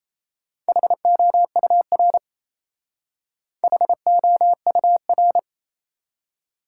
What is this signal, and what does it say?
Morse code, heard as a keyed tone, about 36 wpm, 700 Hz. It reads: HOUR HOUR